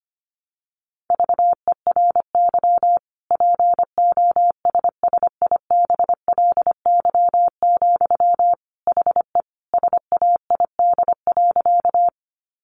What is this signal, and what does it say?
Morse code, heard as a keyed tone, about 25 wpm, 700 Hz. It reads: VERY POSSIBLY, HE SAID.
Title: Sentences from Top 1300 Words 25wpm